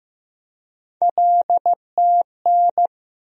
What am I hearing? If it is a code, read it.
Morse code, heard as a keyed tone, about 15 wpm, 700 Hz. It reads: LTN
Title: USA Call Sign Postfixes - 3 Letters 15wpm